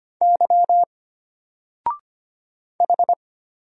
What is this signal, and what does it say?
Morse code, heard as a keyed tone, about 25 wpm, 700 Hz. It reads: Y H
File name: Single-Letter-Number-25wpm